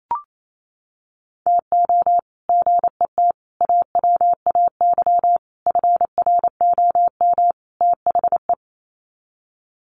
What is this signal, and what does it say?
Morse code, heard as a keyed tone, about 28 wpm, 700 Hz. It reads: TO GET AWAY FROM THE
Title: Sets of 5 Words Encore2 28wpm